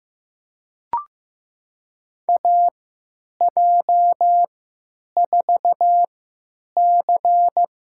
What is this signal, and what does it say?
Morse code, heard as a keyed tone, about 15 wpm, spaced as for 9 wpm, 700 Hz. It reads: AJ4C
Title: USA Call Signs - 3x Character Spacing - 15wpm